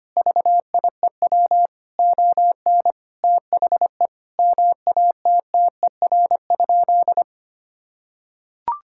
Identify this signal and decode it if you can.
Morse code, heard as a keyed tone, about 25 wpm, 700 Hz. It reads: VIEW ON THE MATTER?
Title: Sentences from Top 1000 Words - Review Part 6 - 25wpm